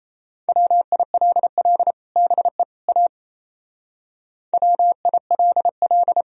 Morse code, heard as a keyed tone, about 34 wpm, 700 Hz. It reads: WILL BE A WILL
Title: Speed-Racing - Sets of 3 Words 28wpm